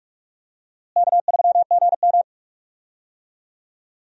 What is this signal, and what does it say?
Morse code, heard as a keyed tone, about 46 wpm, 700 Hz. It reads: K3GM
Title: USA Call Signs - 4 Characters - Repeated 3x - 45wpm